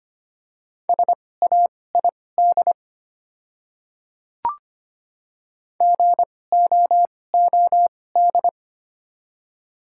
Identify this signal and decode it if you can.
Morse code, heard as a keyed tone, about 25 wpm, spaced as for 18 wpm, 700 Hz. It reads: SAID GOOD
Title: 2x Character Spacing - Top 200 Words - 25wpm